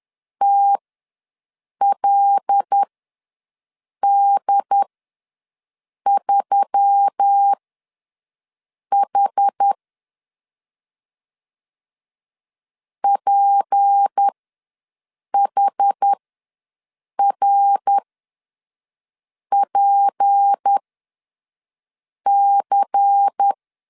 Morse code, heard as a keyed tone, about 10 wpm, spaced as for 5 wpm, 800 Hz. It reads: TLD3H PHRPC
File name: LEZIONI / 6 PKX0 / 6 P ws 10wpm 3751